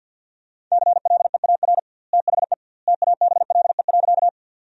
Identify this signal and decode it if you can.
Morse code, heard as a keyed tone, about 50 wpm, 700 Hz. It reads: CLEAR THE TABLE.
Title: 2x Word-Spacing - Sentences from Top 500 Words - 50wpm